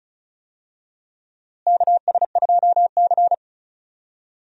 Morse code, heard as a keyed tone, about 36 wpm, 700 Hz. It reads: KS2C
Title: USA Call Signs - 4 Characters - Repeated 2x - 35wpm